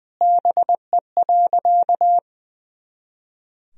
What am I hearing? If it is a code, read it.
Morse code, heard as a keyed tone, about 21 wpm, 700 Hz. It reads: BE.
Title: Sentences from Top 300 Words and Abbreviations 20wpm